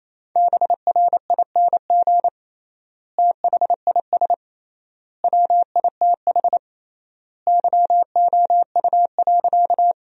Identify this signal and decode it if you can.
Morse code, heard as a keyed tone, about 28 wpm, 700 Hz. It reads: BRING THIS WITH YOU.
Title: 3x Word-Spacing - Sentences from Top 500 Words - 28wpm